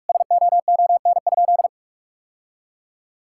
Morse code, heard as a keyed tone, about 44 wpm, 700 Hz. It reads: SOON?
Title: Sentences from Top 1600 Words 45wpm